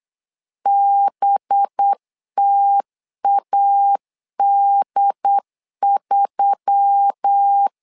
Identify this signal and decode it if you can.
Morse code, heard as a keyed tone, about 8 wpm, 800 Hz. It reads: BTAD3
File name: LEZIONI / 5 LBVC / 5 LB 8wpm 2816